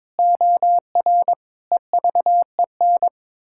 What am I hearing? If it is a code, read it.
Morse code, heard as a keyed tone, about 22 wpm, 700 Hz. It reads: OR EVEN
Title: Pangram Challenge v2 - 22wpm